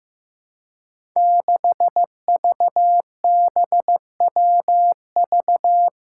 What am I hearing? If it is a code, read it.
Morse code, heard as a keyed tone, about 15 wpm, 700 Hz. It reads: 6VBWV
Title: Five Challenging Letter-Number Combinations 15wpm